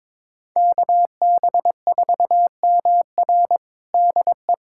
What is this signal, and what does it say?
Morse code, heard as a keyed tone, about 21 wpm, 700 Hz. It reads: KB4MR DE
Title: USA Callsign DE Callsign - 5 Characters - 22wpm